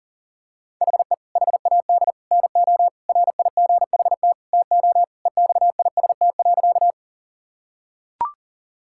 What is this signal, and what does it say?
Morse code, heard as a keyed tone, about 37 wpm, 700 Hz. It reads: HE HAD NO RIGHT TO EXIST.
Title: Sentences from Top 1000 Words - Review Part 5 - 40wpm